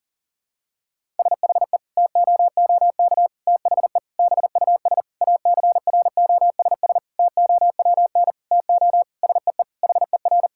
Morse code, heard as a keyed tone, about 40 wpm, 700 Hz. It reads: SHE TOOK THE BUS ACROSS TOWN TO SEE HER
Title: Sentences from Top 1000 Words - Review Part 6 - 40wpm